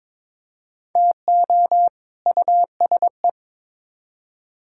Morse code, heard as a keyed tone, about 22 wpm, 700 Hz. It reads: TO USE
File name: Build A Phrase - 2-Word Phrases - 22wpm